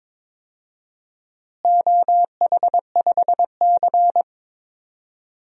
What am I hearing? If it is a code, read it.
Morse code, heard as a keyed tone, about 22 wpm, 700 Hz. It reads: OH5C
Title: DXCC-100 Call Signs - 4 Characters - Repeated 3x - 22wpm